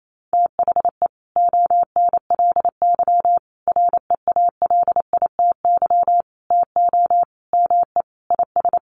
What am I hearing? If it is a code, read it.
Morse code, heard as a keyed tone, about 28 wpm, 700 Hz. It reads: THE ONLY REALITY TO ME IS
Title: Sentences from Top 1200 Words 28wpm